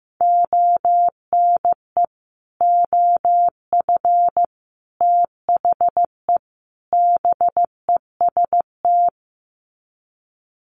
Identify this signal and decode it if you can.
Morse code, heard as a keyed tone, about 15 wpm, 700 Hz. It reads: ONE OF THE BEST
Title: Sets of 4 Words 15wpm